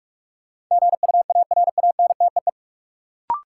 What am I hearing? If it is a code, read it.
Morse code, heard as a keyed tone, about 46 wpm, 700 Hz. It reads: GUARANTEE
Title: Sentences from Top 1600 Words - New Word Drill Alt - 45wpm